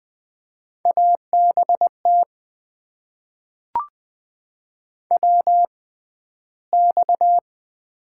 Morse code, heard as a keyed tone, about 20 wpm, spaced as for 12 wpm, 700 Hz. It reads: ABT WX
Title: Top 90 QSO Elements - 6x Character Spacing - 20wpm